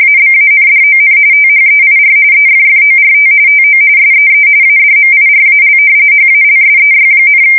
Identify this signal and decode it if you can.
RTTY, about 45.45 baud. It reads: ABANDON THEIR EFFORT TO SEE ADMIRAL HALSEY,
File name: fsk170 / battleshipnj